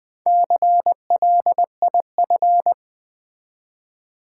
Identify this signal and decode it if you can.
Morse code, heard as a keyed tone, about 20 wpm, 700 Hz. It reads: CLIF
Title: Build A Word - 1 Letter At A Time - 5-Letter Words - 20wpm